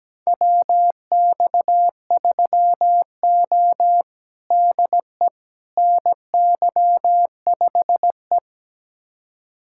Morse code, heard as a keyed tone, about 17 wpm, 700 Hz. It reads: WX3O DE NY5E
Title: USA Callsign DE Callsign - 4 Characters - 17wpm